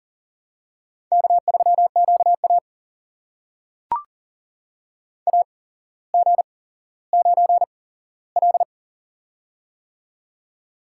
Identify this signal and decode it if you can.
Morse code, heard as a keyed tone, about 39 wpm, spaced as for 17 wpm, 700 Hz. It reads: K3QA AG9L
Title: USA Call Signs - 8x Character Spacing - 40wpm